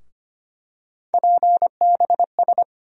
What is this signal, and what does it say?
Morse code, heard as a keyed tone, about 25 wpm, 700 Hz. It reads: PBS
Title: USA Call Sign Postfixes - 3 Letters 25wpm